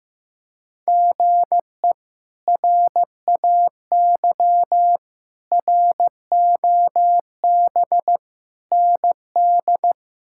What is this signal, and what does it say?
Morse code, heard as a keyed tone, about 15 wpm, 700 Hz. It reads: GE RAY ROB ND
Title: K1USN SST QSO Practice - 15wpm